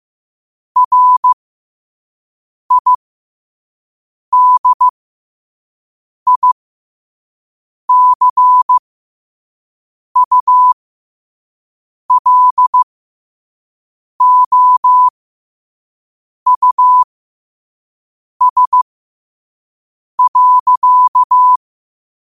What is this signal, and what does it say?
Morse code, heard as a keyed tone, about 15 wpm, spaced as for 5 wpm, 1000 Hz. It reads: RIDICULOUS.
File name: qotdfiles / 5wpm